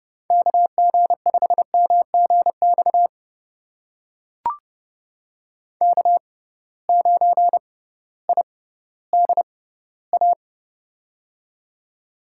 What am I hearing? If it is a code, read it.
Morse code, heard as a keyed tone, about 30 wpm, spaced as for 16 wpm, 700 Hz. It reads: KG5MGX K9IDA
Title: USA Call Signs - 6x Character Spacing - 30wpm